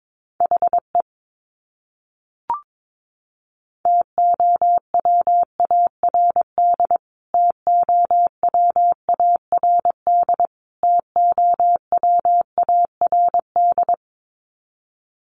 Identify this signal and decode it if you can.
Morse code, heard as a keyed tone, about 22 wpm, 700 Hz. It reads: HE TOWARD TOWARD TOWARD
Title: Top 500 Words Repeated 22wpm